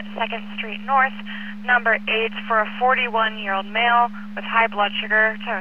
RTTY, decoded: none

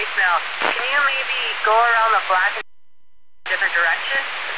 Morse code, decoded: none

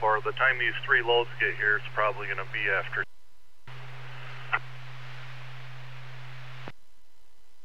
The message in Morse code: none

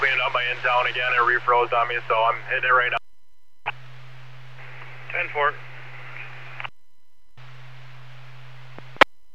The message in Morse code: none

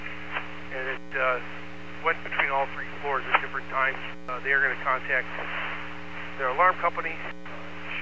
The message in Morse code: none

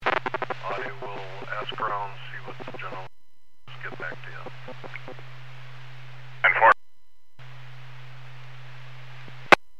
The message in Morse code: none

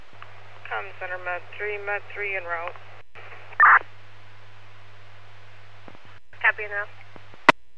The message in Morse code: none

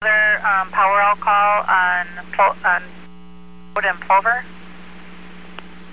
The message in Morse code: none

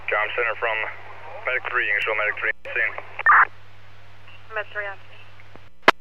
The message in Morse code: none

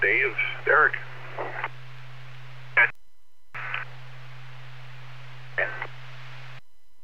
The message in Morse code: none

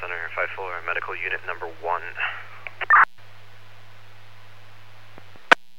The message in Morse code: none

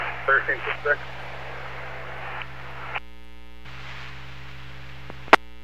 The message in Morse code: none

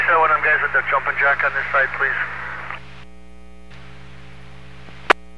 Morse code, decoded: none

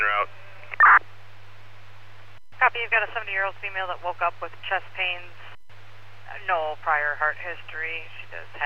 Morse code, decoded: none

